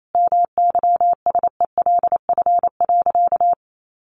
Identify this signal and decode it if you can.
Morse code, heard as a keyed tone, about 28 wpm, 700 Hz. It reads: MYSELF.
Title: Sentences from Top 1300 Words 28wpm